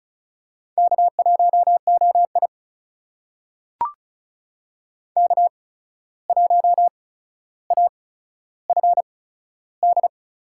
Morse code, heard as a keyed tone, about 36 wpm, spaced as for 14 wpm, 700 Hz. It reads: K1OI K1AFD
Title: USA Call Signs - 8x Character Spacing - 35wpm